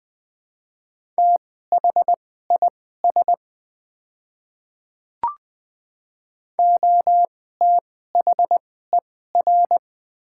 Morse code, heard as a keyed tone, about 20 wpm, spaced as for 14 wpm, 700 Hz. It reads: THIS OTHER
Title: 2x Character Spacing - Top 100 Words - 20wpm